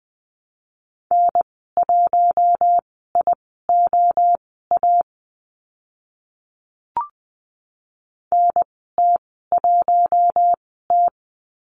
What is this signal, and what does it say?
Morse code, heard as a keyed tone, about 20 wpm, spaced as for 14 wpm, 700 Hz. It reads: N1IOA NT1T
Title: 042 Lesson 8g - Number 1 - Call Signs 20wpm